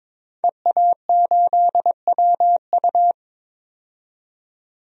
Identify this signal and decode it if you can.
Morse code, heard as a keyed tone, about 22 wpm, 700 Hz. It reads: EA8WU
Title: DXCC-100 Call Signs - 5 Characters - 22wpm